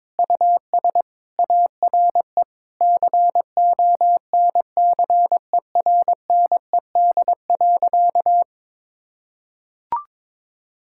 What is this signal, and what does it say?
Morse code, heard as a keyed tone, about 22 wpm, 700 Hz. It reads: US ARE CONCERNED.
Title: Sentences from Top 1000 Words - Review Part 6 - 22wpm